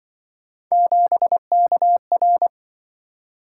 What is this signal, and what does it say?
Morse code, heard as a keyed tone, about 24 wpm, 700 Hz. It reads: 7KR
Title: Speed-Racing - Three Letter-Number Combinations - 20wpm